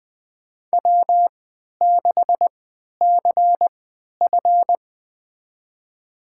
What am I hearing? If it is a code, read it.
Morse code, heard as a keyed tone, about 20 wpm, spaced as for 11 wpm, 700 Hz. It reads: W6CF
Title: USA Call Signs - 4 Characters - 3x Character Spacing - 20wpm